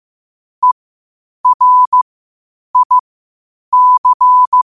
Morse code, heard as a keyed tone, about 15 wpm, spaced as for 8 wpm, 1000 Hz. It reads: ERIC